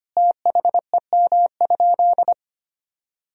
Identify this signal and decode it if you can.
Morse code, heard as a keyed tone, about 25 wpm, 700 Hz. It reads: THEM?
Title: Sentences from Top 1000 Words - Review Part 3 - 25wpm